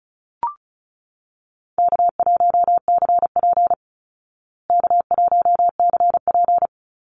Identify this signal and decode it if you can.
Morse code, heard as a keyed tone, about 34 wpm, 700 Hz. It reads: K1CP K1CP
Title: Call Signs Repeated 2x - 35wpm